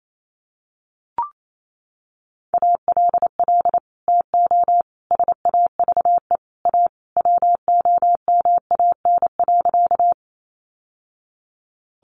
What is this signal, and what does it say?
Morse code, heard as a keyed tone, about 28 wpm, 700 Hz. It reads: ALL TO SAVE A WOMAN.
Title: Sentences from Top 1100 Words 28wpm